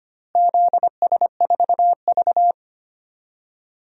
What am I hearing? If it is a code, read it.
Morse code, heard as a keyed tone, about 25 wpm, 700 Hz. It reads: ZS4V
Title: DXCC-100 Call Signs - 4 Characters - Repeated 3x - 25wpm